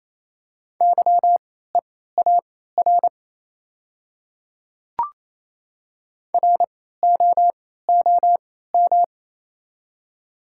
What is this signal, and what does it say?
Morse code, heard as a keyed tone, about 28 wpm, spaced as for 16 wpm, 700 Hz. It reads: YEAR ROOM
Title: 3x Character Spacing - Top 300 Words - 28wpm